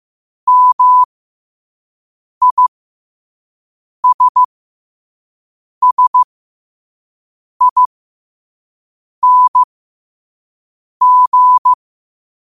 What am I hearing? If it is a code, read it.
Morse code, heard as a keyed tone, about 15 wpm, spaced as for 5 wpm, 1000 Hz. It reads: MISSING
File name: qotdfiles / 5wpm